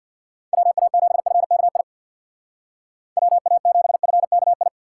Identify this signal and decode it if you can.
Morse code, heard as a keyed tone, about 56 wpm, 700 Hz. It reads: WA7FKI WA7FKI
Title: Speed Racing - USA 2x3 Call Signs 50wpm